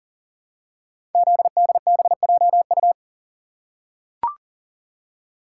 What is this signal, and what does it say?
Morse code, heard as a keyed tone, about 40 wpm, 700 Hz. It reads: ZDBJU
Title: Code Groups - 5 Letters - 40wpm